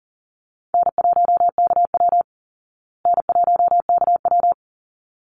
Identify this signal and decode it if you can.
Morse code, heard as a keyed tone, about 40 wpm, 700 Hz. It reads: N1KW N1KW
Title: Easy Call Signs Repeated 2x - 40wpm